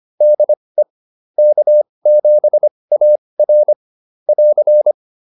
Morse code, heard as a keyed tone, about 25 wpm, spaced as for 20 wpm, 600 Hz. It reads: DE K7AR <AR>